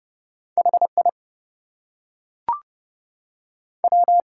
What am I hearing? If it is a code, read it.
Morse code, heard as a keyed tone, about 30 wpm, 700 Hz. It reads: HI W
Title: Two Letter Words - 30wpm